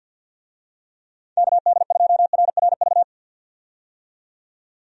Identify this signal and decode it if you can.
Morse code, heard as a keyed tone, about 52 wpm, 700 Hz. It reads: KD2RRV
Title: USA Call Signs - 6 Characters - 50wpm